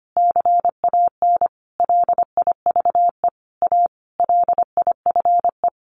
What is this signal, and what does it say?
Morse code, heard as a keyed tone, about 25 wpm, 700 Hz. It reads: CAN LIVE A LIFE
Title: Sentences from Top 1400 Words 25wpm